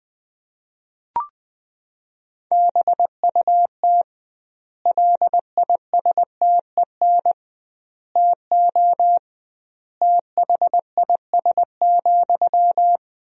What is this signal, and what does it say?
Morse code, heard as a keyed tone, about 20 wpm, 700 Hz. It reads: BUT LISTEN TO THIS,
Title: 2x Word-Spacing - Sentences from Top 400 Words - 20wpm